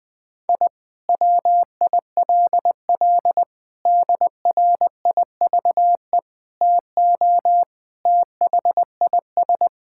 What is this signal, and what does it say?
Morse code, heard as a keyed tone, about 20 wpm, 700 Hz. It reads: I WILL DRIVE TO THIS
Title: Sentences from Top 1000 Words - Review Part 6 - 20wpm